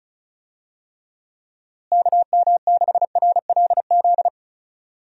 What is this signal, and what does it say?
Morse code, heard as a keyed tone, about 36 wpm, 700 Hz. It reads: KM6RLZ
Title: USA Call Signs - 6 Characters - Repeated 3x - 35wpm